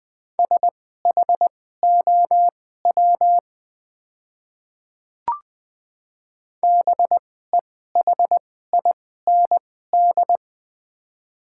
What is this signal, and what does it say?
Morse code, heard as a keyed tone, about 20 wpm, spaced as for 15 wpm, 690 Hz. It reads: SHOW BEHIND
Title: 2x Character Spacing - Top 500 Words - 20wpm